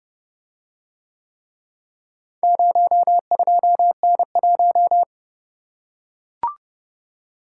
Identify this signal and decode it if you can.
Morse code, heard as a keyed tone, about 29 wpm, 700 Hz. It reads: 02N1